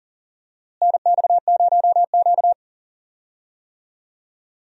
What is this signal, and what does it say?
Morse code, heard as a keyed tone, about 38 wpm, 700 Hz. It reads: NX0Q
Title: USA Call Signs - 4 Characters - Repeated 3x - 40wpm